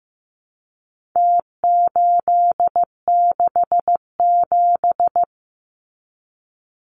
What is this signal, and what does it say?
Morse code, heard as a keyed tone, about 15 wpm, 700 Hz. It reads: T867
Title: Numbers - Contest Style - 4-Digit 15wpm